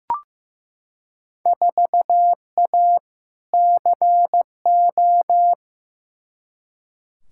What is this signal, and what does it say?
Morse code, heard as a keyed tone, about 15 wpm, 700 Hz. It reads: 4A CO
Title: Field Day Exchanges - 15wpm